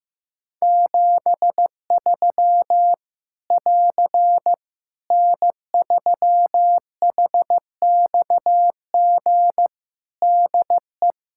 Morse code, heard as a keyed tone, about 15 wpm, 700 Hz. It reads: 73 <AR> N3HXG DE
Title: Practice QSO - Exchange 4 - Ending - 15wpm